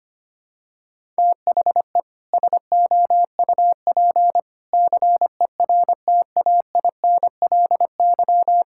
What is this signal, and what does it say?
Morse code, heard as a keyed tone, about 25 wpm, 700 Hz. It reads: THE SOUP CERTAINLY
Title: Sentences from Top 1000 Words - Review Part 6 - 25wpm